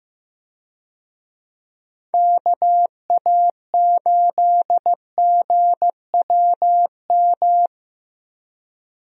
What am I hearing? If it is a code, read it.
Morse code, heard as a keyed tone, about 15 wpm, 700 Hz. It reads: KA8GWM
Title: USA Call Signs - 6 Characters - Repeated 2x - 15wpm